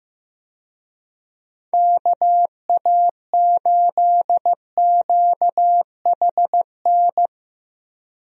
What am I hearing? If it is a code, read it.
Morse code, heard as a keyed tone, about 15 wpm, 700 Hz. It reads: KA8QHN
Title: USA Call Signs - 6 Characters - Repeated 3x - 15wpm